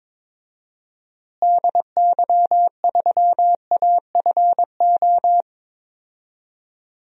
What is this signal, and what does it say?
Morse code, heard as a keyed tone, about 22 wpm, 700 Hz. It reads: DY3AFO
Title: DXCC-100 Call Signs - 6 Characters - Repeated 2x - 22wpm